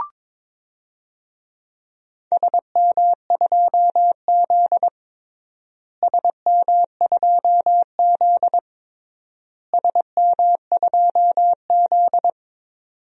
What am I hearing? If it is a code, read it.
Morse code, heard as a keyed tone, about 22 wpm, 700 Hz. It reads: SM2Z SM2Z SM2Z
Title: DXCC-100 Call Signs - 4 Characters - Repeated 3x - 22wpm